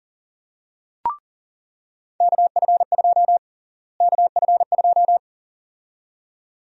Morse code, heard as a keyed tone, about 40 wpm, 700 Hz. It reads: KF2 KF2
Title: USA Call Sign Prefixes - Repeated 2x - 40wpm